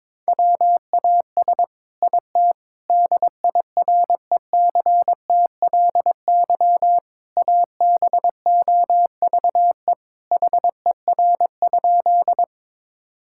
Morse code, heard as a keyed tone, about 21 wpm, 700 Hz. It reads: WAS IT DIRECTLY ABOVE HER?
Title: Sentences from Top 1000 Words - Review Part 5 - 22wpm